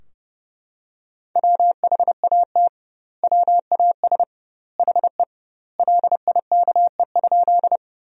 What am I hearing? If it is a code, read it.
Morse code, heard as a keyed tone, about 30 wpm, 700 Hz. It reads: WHAT WAS HE LIKE?
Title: 2x Word-Spacing - Sentences from Top 100 Words - 30wpm